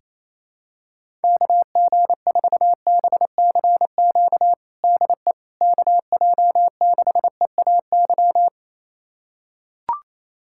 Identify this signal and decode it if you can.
Morse code, heard as a keyed tone, about 28 wpm, 700 Hz. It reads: KG4BCQ DE KJ6EAY
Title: USA Callsign DE Callsign - 6 Characters - 28wpm